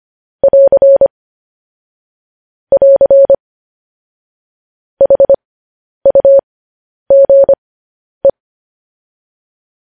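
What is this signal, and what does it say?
Morse code, heard as a keyed tone, about 25 wpm, spaced as for 10 wpm, 550 Hz. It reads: <AR> <AR> HUGE
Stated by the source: Morse code sending of news headlines, speed 10.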